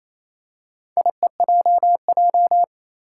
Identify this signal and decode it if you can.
Morse code, heard as a keyed tone, about 28 wpm, 700 Hz. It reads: IEJJ
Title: Four Challenging Letter-Number Combinations 28wpm